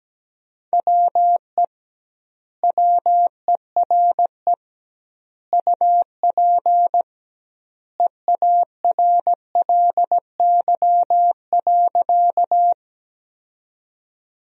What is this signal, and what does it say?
Morse code, heard as a keyed tone, about 17 wpm, 700 Hz. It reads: WE WERE UP EARLY.
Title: 2x Word-Spacing - Sentences from Top 400 Words - 17wpm